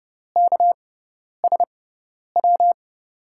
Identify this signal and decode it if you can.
Morse code, heard as a keyed tone, about 31 wpm, spaced as for 10 wpm, 700 Hz. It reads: KSW
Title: USA Call Signs - 6 Characters - 6x Character Spacing - 30wpm